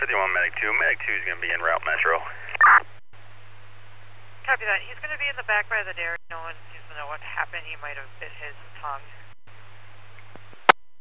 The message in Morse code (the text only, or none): none